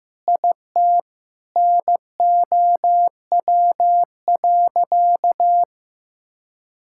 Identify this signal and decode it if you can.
Morse code, heard as a keyed tone, about 15 wpm, 700 Hz. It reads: IT NOW.
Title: Sentences from Top 1000 Words - Review Part 6 - 15wpm